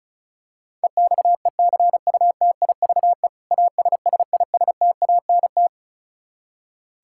Morse code, heard as a keyed tone, about 36 wpm, 700 Hz. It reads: EXECUTIVE ASSISTANT
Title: Practice QSO - Exchange 3 - More Info - 35wpm